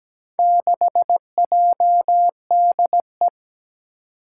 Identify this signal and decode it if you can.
Morse code, heard as a keyed tone, about 17 wpm, 700 Hz. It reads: 6JDE
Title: Four Challenging Letter-Number Combinations 17wpm